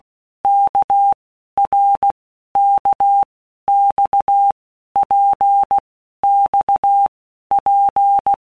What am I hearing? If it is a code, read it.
Morse code, heard as a keyed tone, about 16 wpm, 800 Hz. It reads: K R K X P X P
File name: Basic Letter Groups By Blocks / Groups@15 / RKPX@15